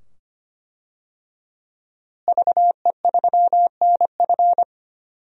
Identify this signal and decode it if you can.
Morse code, heard as a keyed tone, about 25 wpm, 700 Hz. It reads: VE3NF
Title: DXCC-100 Call Signs - 5 Characters - Repeated 2x - 25wpm